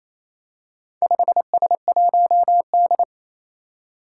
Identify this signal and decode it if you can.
Morse code, heard as a keyed tone, about 29 wpm, 700 Hz. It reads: 5S1D
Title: Four Challenging Letter-Number Combinations 28wpm